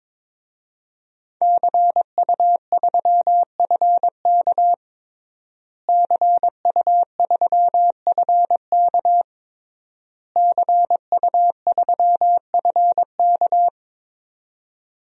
Morse code, heard as a keyed tone, about 22 wpm, 700 Hz. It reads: CU3FK CU3FK CU3FK
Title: DXCC-100 Call Signs - 5 Characters - Repeated 3x - 22wpm